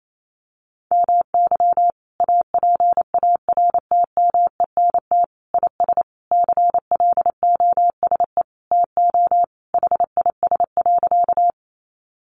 Morse code, heard as a keyed tone, about 28 wpm, 700 Hz. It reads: MY APARTMENT IS CLOSE TO HIS.
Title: Sentences from Top 900 Words 28wpm